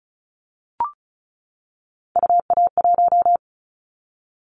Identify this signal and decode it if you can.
Morse code, heard as a keyed tone, about 34 wpm, 700 Hz. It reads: UA1
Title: Callsign Prefixes for Top 25 DXCC - 35wpm